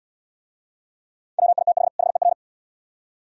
Morse code, heard as a keyed tone, about 72 wpm, 700 Hz. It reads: WITH HER